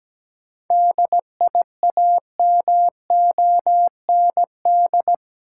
Morse code, heard as a keyed tone, about 17 wpm, 700 Hz. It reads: DIAMOND
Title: Seven Letter Words - 17wpm